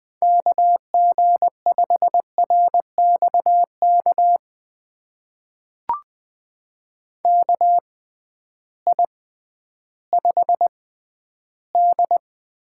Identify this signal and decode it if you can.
Morse code, heard as a keyed tone, about 20 wpm, spaced as for 12 wpm, 700 Hz. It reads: KG5RXK KI5D
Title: USA Call Signs - 6 Characters - 6x Character Spacing - 20wpm